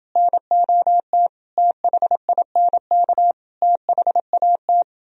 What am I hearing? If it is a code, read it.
Morse code, heard as a keyed tone, about 31 wpm, 700 Hz. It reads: NOT THINK THAT